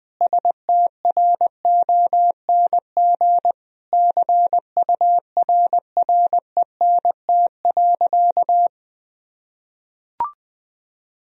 Morse code, heard as a keyed tone, about 20 wpm, 700 Hz. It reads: STRONG CURRENT.
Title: Sentences from Top 1000 Words - Review Part 4 - 20wpm